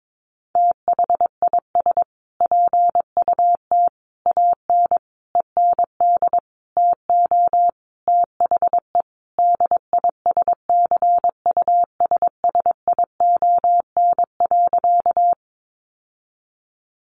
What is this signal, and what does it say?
Morse code, heard as a keyed tone, about 22 wpm, 700 Hz. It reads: THIS PUT AN END TO THE DISCUSSION.
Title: Sentences from Top 1100 Words 22wpm